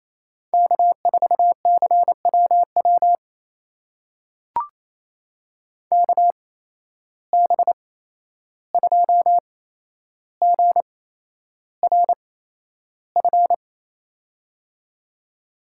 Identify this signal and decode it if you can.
Morse code, heard as a keyed tone, about 28 wpm, spaced as for 11 wpm, 700 Hz. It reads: K4CWW KB2GRF
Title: USA Call Signs - 8x Character Spacing - 28wpm